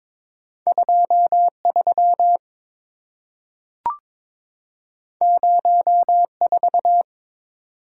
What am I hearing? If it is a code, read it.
Morse code, heard as a keyed tone, about 22 wpm, 700 Hz. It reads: 23 04
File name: practice / 22wpm / Two Letter-Number Combinations - 22wpm